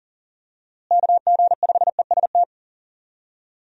Morse code, heard as a keyed tone, about 40 wpm, 700 Hz. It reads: KG5EST
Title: USA Call Signs - 6 Characters - Repeated 3x - 40wpm